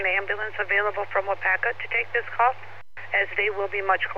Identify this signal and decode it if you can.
Emergency Alert System (SAME) tones, heard as harsh buzzing bursts, with speech. It is heard from 0.3 to 3.4 s.